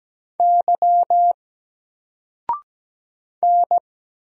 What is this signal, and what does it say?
Morse code, heard as a keyed tone, about 17 wpm, 700 Hz. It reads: Y N
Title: Single Letter-Number Alt 17wpm